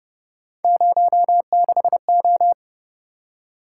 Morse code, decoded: 06O

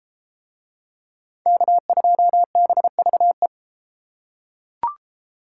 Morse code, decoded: K2BVE

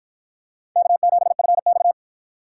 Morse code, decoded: K7FX

